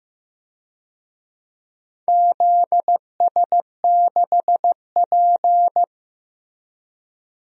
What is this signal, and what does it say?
Morse code, heard as a keyed tone, about 16 wpm, 700 Hz. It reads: ZS6P
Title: DXCC-100 Call Signs - 4 Characters - Repeated 3x - 15wpm